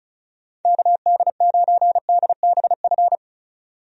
Morse code, heard as a keyed tone, about 36 wpm, 700 Hz. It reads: KD9DBF
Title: USA Call Signs - 6 Characters - Repeated 2x - 35wpm